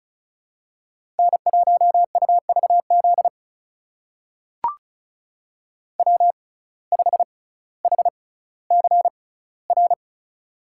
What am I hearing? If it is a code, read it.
Morse code, heard as a keyed tone, about 36 wpm, spaced as for 18 wpm, 700 Hz. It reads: N1UVZ W5HCR
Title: USA Call Signs - 5 Characters - 6x Character Spacing - 35wpm